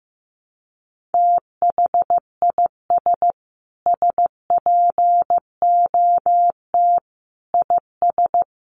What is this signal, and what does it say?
Morse code, heard as a keyed tone, about 15 wpm, 700 Hz. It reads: THIS SPOT IS
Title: Sentences from Top 1100 Words 15wpm